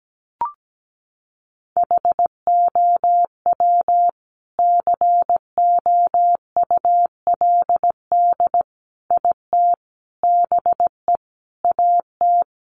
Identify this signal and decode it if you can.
Morse code, heard as a keyed tone, about 17 wpm, 700 Hz. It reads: HOW COULD IT BE AT
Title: Sentences from Top 1400 Words 17wpm